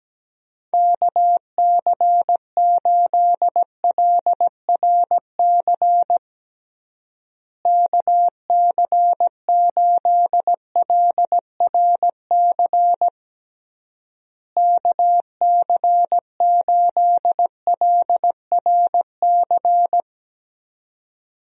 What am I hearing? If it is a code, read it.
Morse code, heard as a keyed tone, about 17 wpm, 700 Hz. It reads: KC8LRC KC8LRC KC8LRC
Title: USA Call Signs - 6 Characters - Repeated 3x - 17wpm